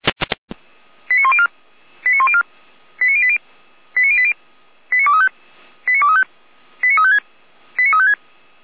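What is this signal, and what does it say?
Five-tone selective call (ZVEI), heard as quick runs of five tones, tones of about 70 ms. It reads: 89104 89104 89080 89080 89235 89235 89356 89356
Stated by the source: ZVEI five-tone call received by SDR.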